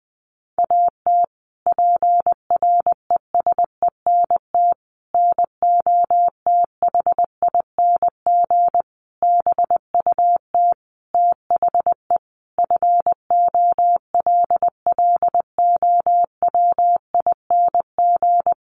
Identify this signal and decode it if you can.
Morse code, heard as a keyed tone, about 20 wpm, 700 Hz. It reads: AT PRESENT NOTHING BUT THE FOLLOWING
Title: Sentences from Top 1500 Words 20wpm